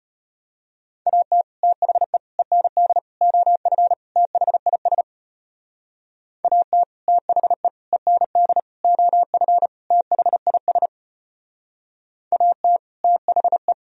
Morse code, heard as a keyed tone, about 33 wpm, 700 Hz. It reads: AT THE END OF THIS AT THE END OF THIS AT THE